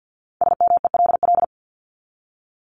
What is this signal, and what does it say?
Morse code, heard as a keyed tone, about 50 wpm, 700 Hz. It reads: SMELL
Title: Top 1501-1750 Words 50wpm